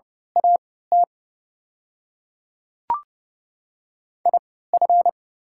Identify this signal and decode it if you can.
Morse code, heard as a keyed tone, about 30 wpm, spaced as for 17 wpm, 700 Hz. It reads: AT IF
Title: 3x Character Spacing - Top 100 Words - 30wpm